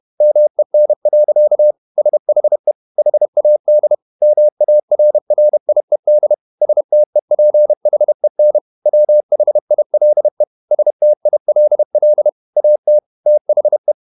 Morse code, heard as a keyed tone, about 31 wpm, 600 Hz. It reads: MEN. SHE HAD MARRIED STEPHEN WHILE STILL AT THE
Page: Never 31wpm | CW med Gnister